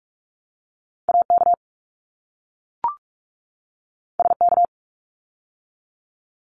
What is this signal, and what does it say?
Morse code, heard as a keyed tone, about 44 wpm, 700 Hz. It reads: AK SK
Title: ARRL Sections 45wpm